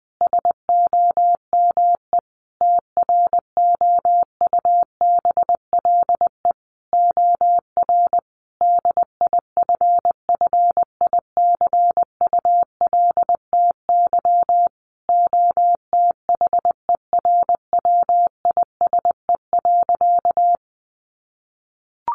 Morse code, heard as a keyed tone, about 20 wpm, 700 Hz. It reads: SOME TROUBLE OR DIFFICULTY OTHERWISE.
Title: Sentences from Top 1500 Words 20wpm